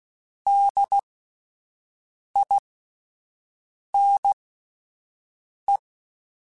Morse code, heard as a keyed tone, about 16 wpm, spaced as for 5 wpm, 800 Hz. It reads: DINE